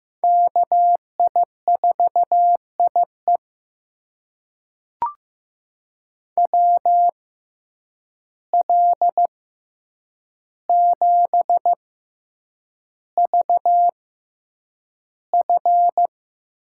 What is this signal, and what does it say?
Morse code, heard as a keyed tone, about 15 wpm, spaced as for 8 wpm, 700 Hz. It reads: KI4IE WL7VF
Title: USA Call Signs - 5 Characters - 6x Character Spacing - 15wpm